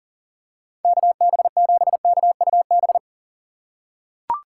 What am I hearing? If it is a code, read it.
Morse code, heard as a keyed tone, about 37 wpm, 700 Hz. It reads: KB7KUB
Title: USA Call Signs - 6 Characters - 4x Character Spacing - 40wpm